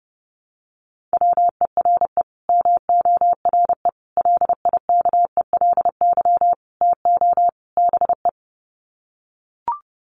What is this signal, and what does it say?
Morse code, heard as a keyed tone, about 30 wpm, 700 Hz. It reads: WERE MORE LIKELY TO BE
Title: Sets of 5 Words Encore2 30wpm